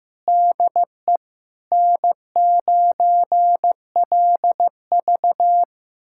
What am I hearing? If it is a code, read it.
Morse code, heard as a keyed tone, about 15 wpm, 700 Hz. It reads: DE N9LV
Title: USA Callsign DE Callsign - 4 Characters - 15wpm